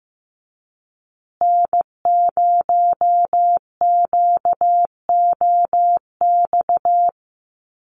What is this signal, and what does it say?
Morse code, heard as a keyed tone, about 15 wpm, 700 Hz. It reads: N0QOX
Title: USA Call Signs Repeated 3x 15wpm